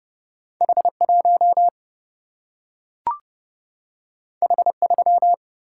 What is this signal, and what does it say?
Morse code, heard as a keyed tone, about 30 wpm, 700 Hz. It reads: H1 H3
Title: Single-Letter Single-Number 30wpm